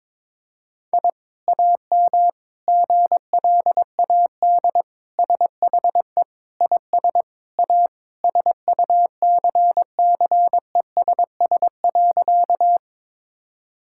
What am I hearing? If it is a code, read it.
Morse code, heard as a keyed tone, about 22 wpm, 700 Hz. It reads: I AM GLAD SHE IS A SUCCESS.
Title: Sentences from Top 1000 Words - Review Part 6 - 22wpm